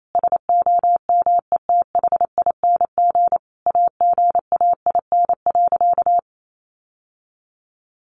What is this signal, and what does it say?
Morse code, heard as a keyed tone, about 28 wpm, 700 Hz. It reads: SOMETHING AGAIN.
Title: Sentences from Top 1100 Words 28wpm